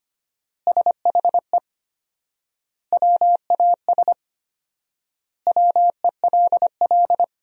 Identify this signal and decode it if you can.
Morse code, heard as a keyed tone, about 25 wpm, 700 Hz. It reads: SHE WAS WELL
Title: 4x Word-Spacing - Sentences from Top 400 Words - 25wpm